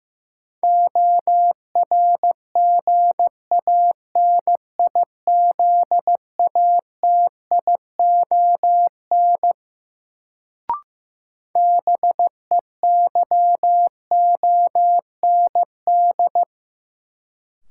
Morse code, heard as a keyed tone, about 15 wpm, 700 Hz. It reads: ORGANIZATION BEYOND
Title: Sentences from Top 1100 Words - New Word Drill Alt - 15wpm